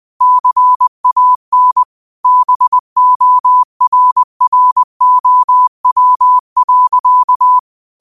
Morse code, heard as a keyed tone, about 20 wpm, 1000 Hz. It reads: CAN BORROW.
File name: qotdfiles / 20wpm